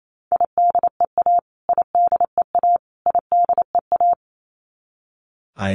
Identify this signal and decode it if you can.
Morse code, heard as a keyed tone, about 28 wpm, 700 Hz. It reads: IDEA IDEA IDEA
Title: Top 300 Words Repeated 28wpm